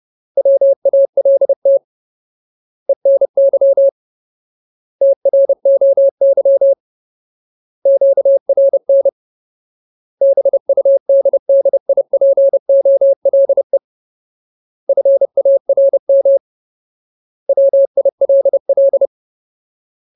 Morse code, heard as a keyed tone, about 30 wpm, 550 Hz. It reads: WALT ENY TROY QRN BUDDIPOLE FARM WILL
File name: Ultimate Word List / 30WPM Ultimate